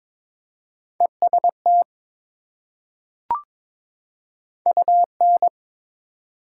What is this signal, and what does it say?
Morse code, heard as a keyed tone, about 21 wpm, 700 Hz. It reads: EST UN
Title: Prefixes and Suffixes 22wpm